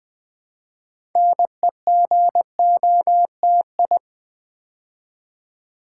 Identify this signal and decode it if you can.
Morse code, heard as a keyed tone, about 20 wpm, 700 Hz. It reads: NEGOTI
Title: Build A Word - 3 Letters At A Time - 9-Letter Words - 20wpm